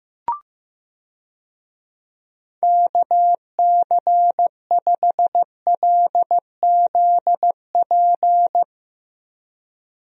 Morse code, heard as a keyed tone, about 15 wpm, 700 Hz. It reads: KC5LZP
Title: USA Call Signs - 6 Characters - Repeated 2x - 15wpm